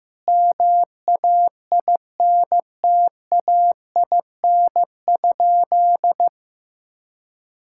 Morse code, heard as a keyed tone, about 15 wpm, 700 Hz. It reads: MAINTAIN?
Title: Sentences from Top 1000 Words - Review Part 5 - 15wpm